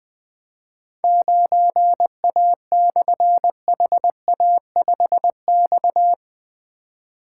Morse code, heard as a keyed tone, about 20 wpm, 700 Hz. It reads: 9A/HA5X
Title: Difficult Call Signs 20wpm